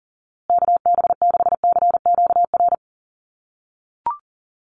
Morse code, heard as a keyed tone, about 40 wpm, 700 Hz. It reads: KB6CQR
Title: USA Call Signs Repeated 3x 40wpm